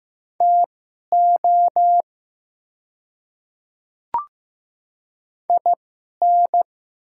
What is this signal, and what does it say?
Morse code, heard as a keyed tone, about 15 wpm, spaced as for 11 wpm, 700 Hz. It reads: TO IN
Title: 2x Character Spacing - Top 300 Words - 15wpm